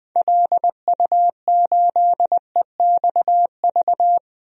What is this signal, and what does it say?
Morse code, heard as a keyed tone, about 21 wpm, 700 Hz. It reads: LU8EXV
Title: DXCC-100 Call Signs - 6 Characters - 20wpm